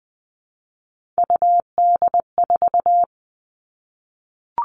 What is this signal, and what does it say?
Morse code, heard as a keyed tone, about 20 wpm, 700 Hz. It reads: UD4
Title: All Cut Numbers - 3-Digit 20wpm